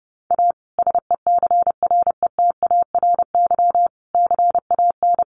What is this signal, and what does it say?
Morse code, heard as a keyed tone, about 30 wpm, 700 Hz. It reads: A SECRETARY CAN
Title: Sentences from Top 1300 Words 30wpm